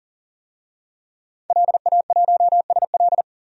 Morse code, heard as a keyed tone, about 40 wpm, 700 Hz. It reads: LA1SL
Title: DXCC-100 Call Signs - 5 Characters - 40wpm